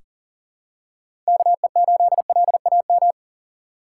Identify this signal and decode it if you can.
Morse code, heard as a keyed tone, about 39 wpm, 700 Hz. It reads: KE8LAM
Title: USA Call Signs - 6 Characters - 2x Character Spacing - 40wpm